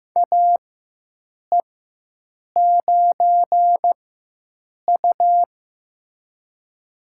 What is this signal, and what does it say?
Morse code, heard as a keyed tone, about 16 wpm, spaced as for 7 wpm, 700 Hz. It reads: AE9U
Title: USA Call Signs - 4x Character Spacing - 15wpm